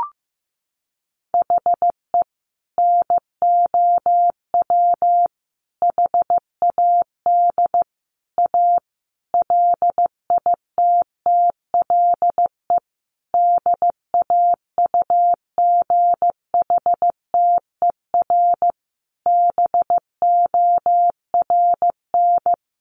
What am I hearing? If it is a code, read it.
Morse code, heard as a keyed tone, about 15 wpm, 700 Hz. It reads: HE NOW HAD A LITTLE DAUGHTER BORN
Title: Sentences from Top 1500 Words 15wpm